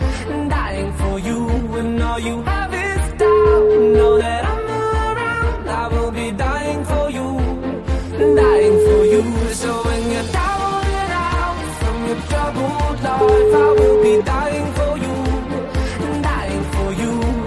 Ringback tone (Europe, 425 Hz), with noise.